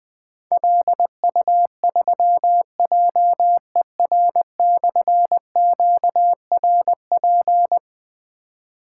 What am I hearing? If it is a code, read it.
Morse code, heard as a keyed tone, about 20 wpm, 700 Hz. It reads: LU3JER/QRP
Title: Difficult Call Signs 20wpm